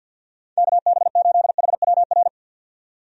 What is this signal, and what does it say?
Morse code, heard as a keyed tone, about 52 wpm, 700 Hz. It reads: KB8HWR
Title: USA Call Signs - 6 Characters - 50wpm